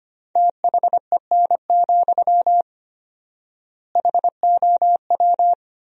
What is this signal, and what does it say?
Morse code, heard as a keyed tone, about 25 wpm, 700 Hz. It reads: THEN, HOW
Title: 4x Word-Spacing - Sentences from Top 100 Words - 25wpm